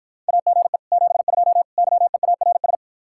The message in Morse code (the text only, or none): AGE 72 YEARS